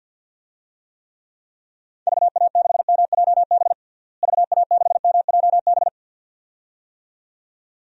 Morse code, heard as a keyed tone, about 52 wpm, 700 Hz. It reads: VA6MJB VA6MJB
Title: DXCC-100 Call Signs - 6 Characters - Repeated 2x - 50wpm